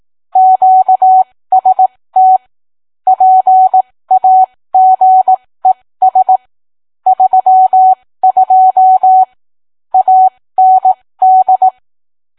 Morse code, heard as a keyed tone, about 18 wpm, spaced as for 15 wpm, 750 Hz. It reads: QST PAGES 32 AND